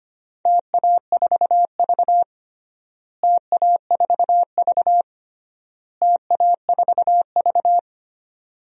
Morse code, heard as a keyed tone, about 25 wpm, 700 Hz. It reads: TA4V TA4V TA4V
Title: DXCC-100 Call Signs - 4 Characters - Repeated 3x - 25wpm